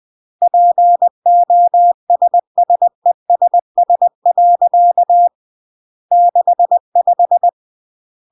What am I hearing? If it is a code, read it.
Morse code, heard as a keyed tone, about 20 wpm, 700 Hz. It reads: POSSESS. 65
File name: CW / jodorowsky / 82 commandments 20-20